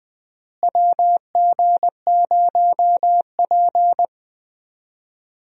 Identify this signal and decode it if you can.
Morse code, heard as a keyed tone, about 20 wpm, 700 Hz. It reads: WG0P
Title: USA Call Signs - 4 Characters - Repeated 3x - 20wpm